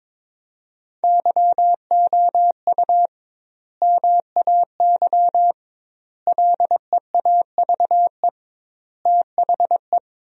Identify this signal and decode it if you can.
Morse code, heard as a keyed tone, about 22 wpm, 700 Hz. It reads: YOU MAY LEAVE THE
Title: 2x Word-Spacing - Sentences from Top 400 Words - 22wpm